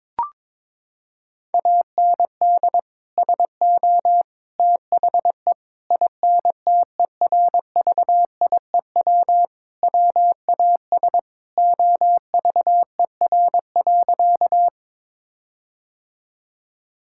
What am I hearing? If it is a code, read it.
Morse code, heard as a keyed tone, about 23 wpm, 700 Hz. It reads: AND SO THE INTERVIEW WAS OVER.
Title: Sentences from Top 1600 Words 22wpm